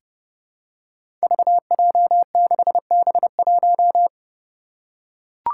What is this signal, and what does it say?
Morse code, heard as a keyed tone, about 30 wpm, 700 Hz. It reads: VJ6B1
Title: Five Challenging Letter-Number Combinations 30wpm